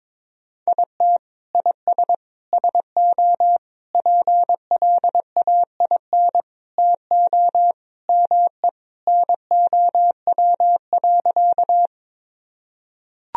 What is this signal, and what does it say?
Morse code, heard as a keyed tone, about 21 wpm, 700 Hz. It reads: IT IS SO PLAIN TO ME NOW.
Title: Sentences from Top 1000 Words - Review Part 2 - 22wpm